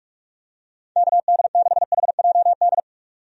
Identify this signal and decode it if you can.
Morse code, heard as a keyed tone, about 46 wpm, 700 Hz. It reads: KD6HJD
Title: USA Call Signs - 6 Characters - 45wpm